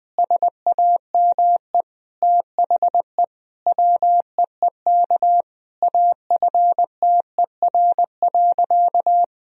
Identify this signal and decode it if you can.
Morse code, heard as a keyed tone, about 20 wpm, 700 Hz. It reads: SAME THE WEEK AFTER.
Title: Sentences from Top 500 Words and Abbreviations 20wpm